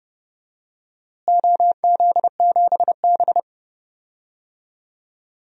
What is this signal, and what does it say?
Morse code, heard as a keyed tone, about 31 wpm, 700 Hz. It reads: OZ7B
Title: DXCC-100 Call Signs - 4 Characters - Repeated 3x - 30wpm